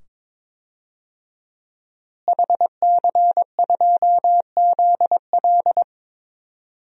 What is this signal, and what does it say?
Morse code, heard as a keyed tone, about 22 wpm, 700 Hz. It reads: HC2ZL
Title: DXCC-100 Call Signs - 5 Characters - Repeated 2x - 22wpm